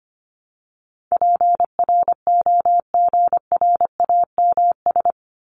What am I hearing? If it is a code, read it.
Morse code, heard as a keyed tone, about 25 wpm, 700 Hz. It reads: PROGRAMS
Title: ES:S Suffix - Top 100 Words with Base Words 25wpm